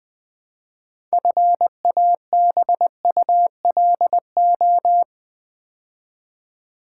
Morse code, heard as a keyed tone, about 21 wpm, 700 Hz. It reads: FABULO
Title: Build A Word - 2 Letters At A Time - 8-Letter Words - 20wpm